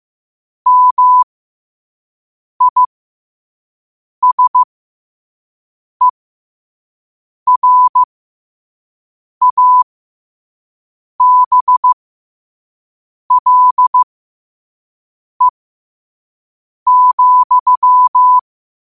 Morse code, heard as a keyed tone, about 15 wpm, spaced as for 5 wpm, 1000 Hz. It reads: MISERABLE,
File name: qotdfiles / 5wpm